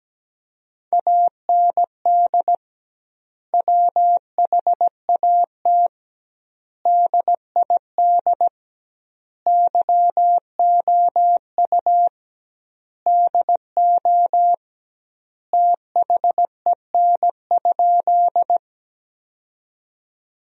Morse code, heard as a keyed tone, about 17 wpm, 700 Hz. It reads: AND WHAT DID YOU DO THEN?
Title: 2x Word-Spacing - Sentences from Top 100 Words - 17wpm